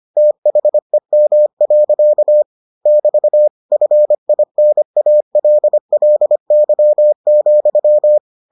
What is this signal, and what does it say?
Morse code, heard as a keyed tone, about 25 wpm, 600 Hz. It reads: THEM. <BT> FINALLY,